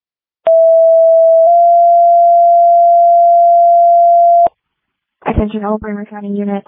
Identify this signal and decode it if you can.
Two-tone page, heard as a one-second tone, then a three-second tone, about 660 then 690 Hz, with speech. It is heard from 0.5 to 4.5 s.